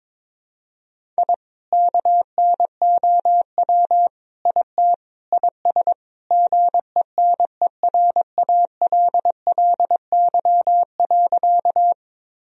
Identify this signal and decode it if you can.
Morse code, heard as a keyed tone, about 22 wpm, 700 Hz. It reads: I KNOW IT IS GENERALLY.
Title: Sentences from Top 1000 Words - Review Part 4 - 22wpm